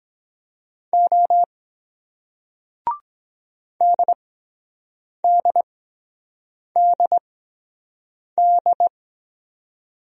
Morse code, heard as a keyed tone, about 23 wpm, spaced as for 7 wpm, 700 Hz. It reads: O DDDD